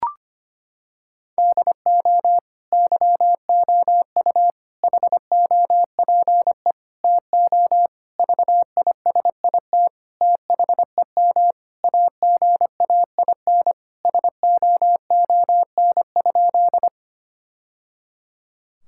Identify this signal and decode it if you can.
Morse code, heard as a keyed tone, about 25 wpm, 700 Hz. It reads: DO YOU HOPE TO VISIT THEM AGAIN SOON?
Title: Sentences from Top 1000 Words - Review Part 6 - 25wpm